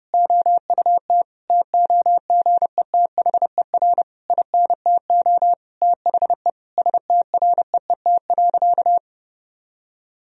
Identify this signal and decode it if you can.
Morse code, heard as a keyed tone, about 30 wpm, 700 Hz. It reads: OUT TOGETHER INTO THE STREET.
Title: Sentences from Top 1000 Words - Review Part 3 - 30wpm